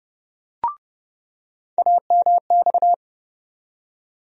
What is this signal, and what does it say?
Morse code, heard as a keyed tone, about 30 wpm, 700 Hz. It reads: AMX
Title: USA Call Sign Postfixes - 3 Letters 30wpm